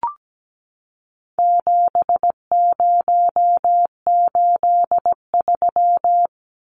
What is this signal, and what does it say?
Morse code, heard as a keyed tone, about 17 wpm, 700 Hz. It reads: 7083
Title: Numbers - 4-Digit 17wpm